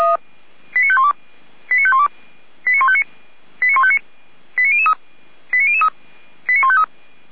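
Five-tone selective call (ZVEI), heard as quick runs of five tones, tones of about 70 ms. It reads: 89512 89512 89159 89159 89003 89003 89153
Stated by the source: ZVEI five-tone selective call received on SDR.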